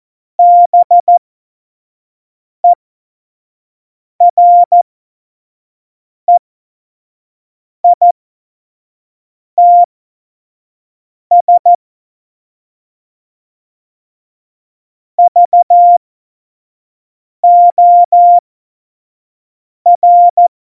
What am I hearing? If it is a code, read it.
Morse code, heard as a keyed tone, about 14 wpm, spaced as for 5 wpm, 700 Hz. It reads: BEREITS VOR
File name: Morsekurs / Witze / Witz 24. September 2019, 25 F70